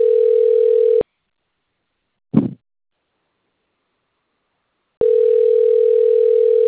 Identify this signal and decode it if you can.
Ringback tone (North America), 440 + 480 Hz.
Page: Atibaia - SP